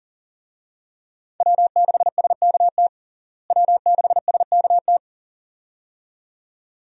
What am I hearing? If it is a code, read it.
Morse code, heard as a keyed tone, about 40 wpm, 700 Hz. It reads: W6SKT W6SKT
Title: USA Call Signs - 5 Characters - Repeated 2x - 40wpm